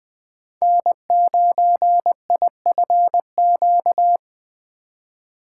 Morse code, heard as a keyed tone, about 21 wpm, 700 Hz. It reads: N9IFQ
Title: USA Call Signs - 3x Character Spacing - 20wpm